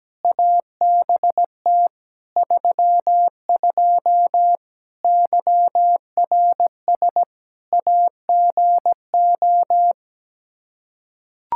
Morse code, heard as a keyed tone, about 17 wpm, 700 Hz. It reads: ABT 32 YRS AGO
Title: Practice QSO - Exchange 3 - More Info - 17wpm